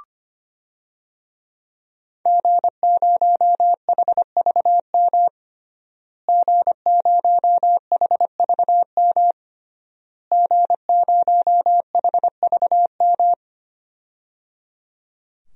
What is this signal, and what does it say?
Morse code, heard as a keyed tone, about 25 wpm, 700 Hz. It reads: G0HVM G0HVM G0HVM
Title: DXCC-100 Call Signs - 5 Characters - Repeated 3x - 25wpm